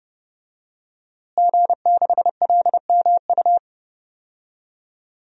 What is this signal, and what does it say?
Morse code, heard as a keyed tone, about 32 wpm, 700 Hz. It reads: G6LMU
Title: DXCC-100 Call Signs - 5 Characters - Repeated 3x - 30wpm